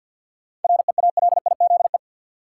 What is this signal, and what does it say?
Morse code, heard as a keyed tone, about 47 wpm, 700 Hz. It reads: REALIZE
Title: Sentences from Top 1000 Words - New Word Drill - 50wpm